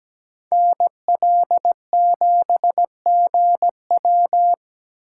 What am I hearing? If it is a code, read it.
Morse code, heard as a keyed tone, about 20 wpm, 700 Hz. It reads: NL7GW
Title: Speed Racing - USA 2x2 Call Signs 17wpm